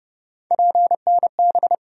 Morse code, heard as a keyed tone, about 30 wpm, 700 Hz. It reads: PNB
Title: USA Call Sign Postfixes - 3 Letters 30wpm